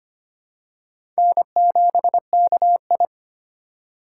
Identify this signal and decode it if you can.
Morse code, heard as a keyed tone, about 25 wpm, 700 Hz. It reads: N7KI
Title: USA Call Signs - 4 Characters - 4x Character Spacing - 25wpm